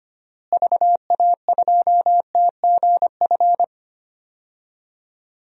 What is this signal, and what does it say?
Morse code, heard as a keyed tone, about 25 wpm, 700 Hz. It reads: VA2TGF
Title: DXCC-100 Call Signs - 6 Characters - Repeated 3x - 25wpm